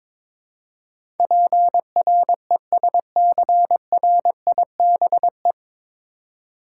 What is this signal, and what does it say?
Morse code, heard as a keyed tone, about 21 wpm, 700 Hz. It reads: PRESCRIBE
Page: Morse Code Ninja practice 22wpm